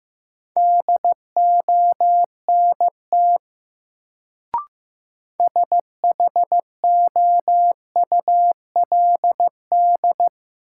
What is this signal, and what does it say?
Morse code, heard as a keyed tone, about 15 wpm, 700 Hz. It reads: DONT SHOULD
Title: Sentences from Top 300 Words - New Word Drill - 15wpm